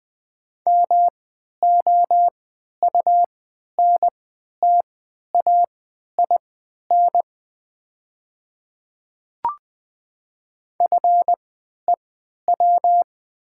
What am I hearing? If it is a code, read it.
Morse code, heard as a keyed tone, about 20 wpm, spaced as for 11 wpm, 690 Hz. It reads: MOUNTAIN FEW
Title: 3x Character Spacing - Top 300 Words - 20wpm